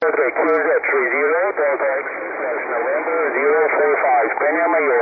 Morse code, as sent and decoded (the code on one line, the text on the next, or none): none